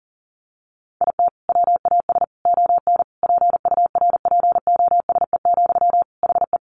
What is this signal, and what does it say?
Morse code, heard as a keyed tone, about 40 wpm, 700 Hz. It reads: IT WAS ON PURPOSE, HE
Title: Sentences from Top 800 Words 40wpm